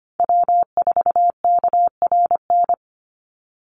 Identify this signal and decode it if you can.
Morse code, heard as a keyed tone, about 25 wpm, 700 Hz. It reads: W4KRN
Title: NA SOTA Chaser Callsigns 25wpm